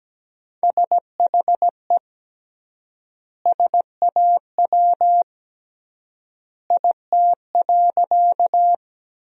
Morse code, heard as a keyed tone, about 17 wpm, 700 Hz. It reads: SHE SAW IT.
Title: 3x Word-Spacing - Sentences from Top 300 Words - 17wpm